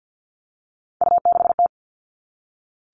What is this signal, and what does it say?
Morse code, heard as a keyed tone, about 48 wpm, 700 Hz. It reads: U6T
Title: All Cut Numbers - 3-Digit 50wpm